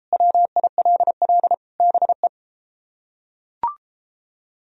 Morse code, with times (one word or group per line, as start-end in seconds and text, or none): 0.1-1.5 WILL
1.8-2.3 BE